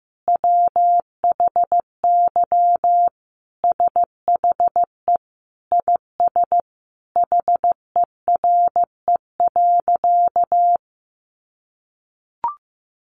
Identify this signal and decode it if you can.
Morse code, heard as a keyed tone, about 15 wpm, 700 Hz. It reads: WHY SHE IS HERE.
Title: Sentences from Top 600 Words 15wpm